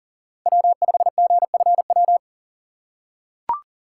Morse code, heard as a keyed tone, about 38 wpm, 700 Hz. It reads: W5GFW
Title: USA Call Signs - 2x Character Spacing - 40wpm